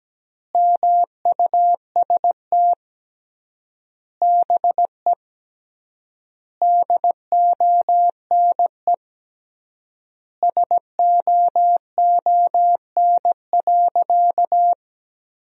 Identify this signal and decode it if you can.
Morse code, heard as a keyed tone, about 17 wpm, 700 Hz. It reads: MUST BE DONE SOON.
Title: 3x Word-Spacing - Sentences from Top 500 Words - 17wpm